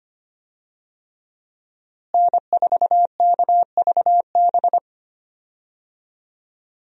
Morse code, heard as a keyed tone, about 25 wpm, 700 Hz. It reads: N4KVB
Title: USA Call Signs - 5 Characters - Repeated 3x - 25wpm